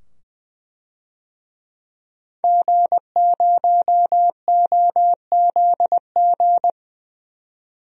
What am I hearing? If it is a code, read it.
Morse code, heard as a keyed tone, about 21 wpm, 700 Hz. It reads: G0OZG
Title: DXCC-100 Call Signs - 5 Characters - 20wpm